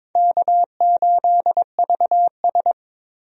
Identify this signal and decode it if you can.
Morse code, heard as a keyed tone, about 22 wpm, 700 Hz. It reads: K8VS
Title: USA Call Signs - 4 Characters - Repeated 3x - 22wpm